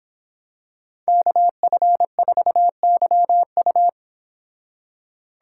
Morse code, heard as a keyed tone, about 26 wpm, 700 Hz. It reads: KF4YU